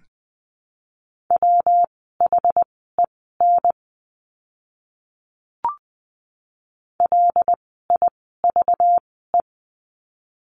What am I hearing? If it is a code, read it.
Morse code, heard as a keyed tone, about 19 wpm, spaced as for 14 wpm, 700 Hz. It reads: WHEN LIVE
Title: 244 Lesson 39f - Letter z - Words 20wpm